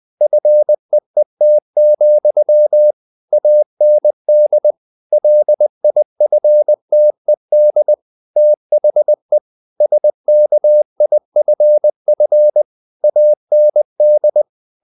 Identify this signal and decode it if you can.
Morse code, heard as a keyed tone, about 20 wpm, 600 Hz. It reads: FEET, AND LIFTED THE SKIFF AND
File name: the old man and the sea / the old man and the sea part 1 20wpm